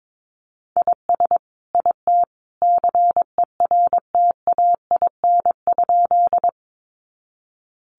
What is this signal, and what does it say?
Morse code, heard as a keyed tone, about 22 wpm, 700 Hz. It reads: IS IT CERTAIN?
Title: Sentences from Top 500 Words 22wpm